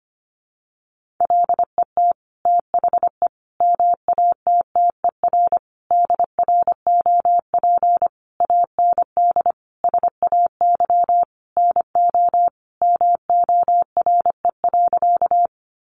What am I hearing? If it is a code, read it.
Morse code, heard as a keyed tone, about 25 wpm, 700 Hz. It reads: LET THE MATTER DROP AND SAY NO MORE.
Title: Sentences from Top 1400 Words 25wpm